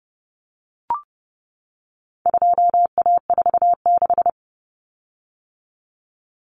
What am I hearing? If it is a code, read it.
Morse code, heard as a keyed tone, about 28 wpm, 700 Hz. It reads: 2A46
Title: Common Cut Numbers - 4-Digit 30wpm